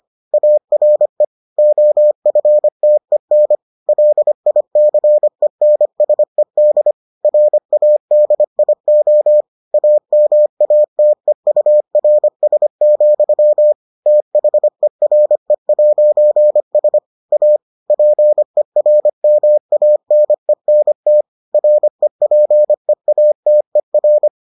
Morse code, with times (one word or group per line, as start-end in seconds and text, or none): 0.3-1.2 ARE
1.6-3.6 OFTEN
3.9-6.9 LICENSED
7.2-9.4 RADIO
9.7-13.7 AMATEURS,
14.1-17.0 THERE'S
17.3-17.6 A
17.9-21.2 PERMANENT
21.5-24.3 REPEATER